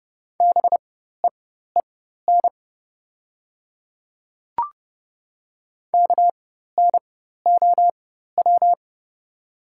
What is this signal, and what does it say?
Morse code, heard as a keyed tone, about 30 wpm, spaced as for 14 wpm, 690 Hz. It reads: BEEN KNOW
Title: 4x Character Spacing - Top 100 Words - 30wpm